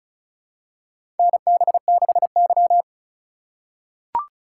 Morse code, heard as a keyed tone, about 34 wpm, 700 Hz. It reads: NB6Y
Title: USA Call Signs - 4 Characters - 2x Character Spacing - 35wpm